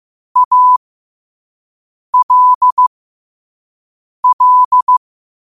Morse code, heard as a keyed tone, about 15 wpm, spaced as for 5 wpm, 1000 Hz. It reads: ALL
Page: Quotes for Sat, 16 Aug 2025 in Morse Code at 5 words per minute.